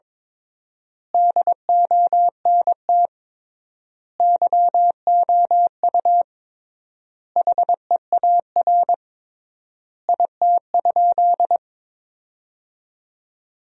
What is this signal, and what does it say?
Morse code, heard as a keyed tone, about 22 wpm, 700 Hz. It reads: DONT YOU HEAR IT?
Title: 3x Word-Spacing - Sentences from Top 300 Words - 22wpm